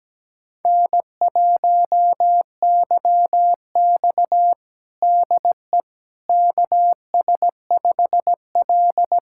Morse code, heard as a keyed tone, about 17 wpm, 700 Hz. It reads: N1YX DE KS5L
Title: USA Callsign DE Callsign - 4 Characters - 17wpm